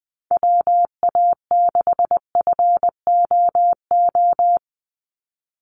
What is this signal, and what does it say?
Morse code, heard as a keyed tone, about 20 wpm, 700 Hz. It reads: WA6FOO
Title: USA Call Signs 20wpm